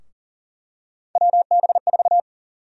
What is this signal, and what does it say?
Morse code, heard as a keyed tone, about 40 wpm, 700 Hz. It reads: WB4
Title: USA Call Sign Prefixes - Repeated 2x - 40wpm